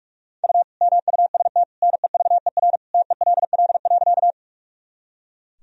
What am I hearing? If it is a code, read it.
Morse code, heard as a keyed tone, about 41 wpm, 700 Hz. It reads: U MUST NEVER TELL.
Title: Sentences from Top 300 Words and Abbreviations 45wpm